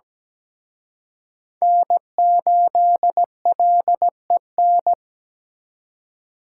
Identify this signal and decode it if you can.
Morse code, heard as a keyed tone, about 20 wpm, 700 Hz. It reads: N8LEN
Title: Speed Racing - USA Call Signs 17wpm